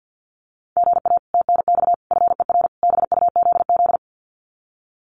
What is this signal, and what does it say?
Morse code, heard as a keyed tone, about 50 wpm, 700 Hz. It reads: GA TNX FER BUZZ
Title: Common QSO Phrases Encore 50wpm